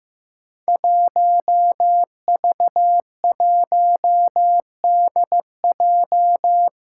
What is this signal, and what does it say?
Morse code, heard as a keyed tone, about 15 wpm, 700 Hz. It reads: 1V1DJ
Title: Five Challenging Letter-Number Combinations 15wpm